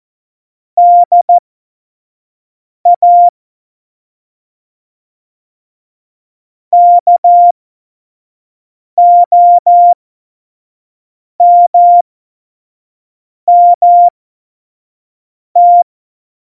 Morse code, heard as a keyed tone, about 14 wpm, spaced as for 5 wpm, 700 Hz. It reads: DA KOMMT
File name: Morsekurs / Witze / Witz 13. November 2018, 25 F70